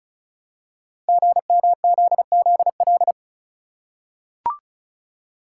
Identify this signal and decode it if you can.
Morse code, heard as a keyed tone, about 36 wpm, 700 Hz. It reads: GMZZL
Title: Code Groups - 5 Letters - 35wpm